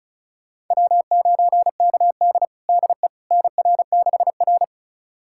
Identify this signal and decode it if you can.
Morse code, heard as a keyed tone, about 36 wpm, 700 Hz. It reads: W9KD DE NR6R
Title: USA Callsign DE Callsign - 4 Characters - 35wpm